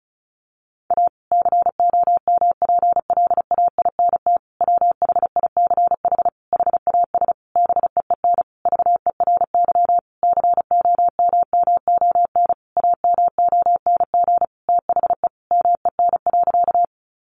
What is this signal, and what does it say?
Morse code, heard as a keyed tone, about 34 wpm, 700 Hz. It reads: A COMPLAINT WHICH HAS BEEN VERY COMMON AMONG THE MEN.
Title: Sentences from Top 1400 Words 35wpm